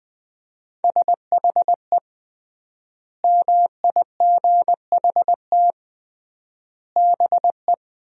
Morse code, heard as a keyed tone, about 21 wpm, 700 Hz. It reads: SHE MIGHT BE
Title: 3x Word-Spacing - Sentences from Top 300 Words - 20wpm